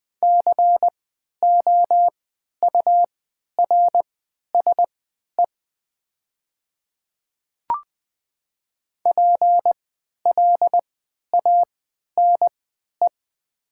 Morse code, heard as a keyed tone, about 20 wpm, spaced as for 11 wpm, 700 Hz. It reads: COURSE PLANE
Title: 3x Character Spacing - Top 500 Words - 20wpm